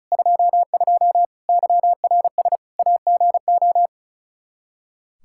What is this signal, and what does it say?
Morse code, heard as a keyed tone, about 36 wpm, 700 Hz. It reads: 22 YRS AGO
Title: Practice QSO - Exchange 3 - More Info - 35wpm